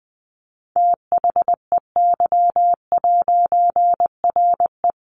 Morse code, heard as a keyed tone, about 20 wpm, 700 Hz. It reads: THEY'RE
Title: A Sound Of Thunder 20wpm